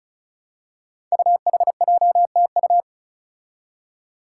Morse code, heard as a keyed tone, about 36 wpm, 700 Hz. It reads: UHJTU
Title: Five Challenging Letter-Number Combinations 35wpm